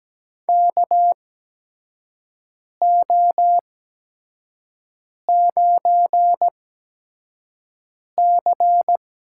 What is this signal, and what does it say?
Morse code, heard as a keyed tone, about 17 wpm, spaced as for 5 wpm, 700 Hz. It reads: KO9C
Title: USA Call Signs - 8x Character Spacing - 17wpm